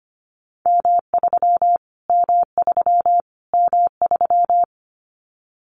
Morse code, heard as a keyed tone, about 25 wpm, 700 Hz. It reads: M3 M3 M3
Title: Callsign Prefixes for Top 25 DXCC - Repeated 3x - 25wpm